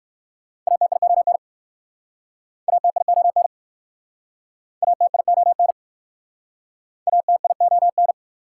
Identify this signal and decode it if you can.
Morse code, heard as a keyed tone, about 60 wpm, 700 Hz. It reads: ATION ATION ATION ATION